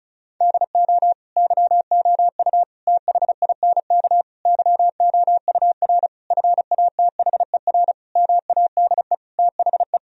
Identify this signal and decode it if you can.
Morse code, heard as a keyed tone, about 36 wpm, 700 Hz. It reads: DO YOU THINK YOUR FATHER MADE THE
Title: Sentences from Top 1000 Words - Review Part 1 - 35wpm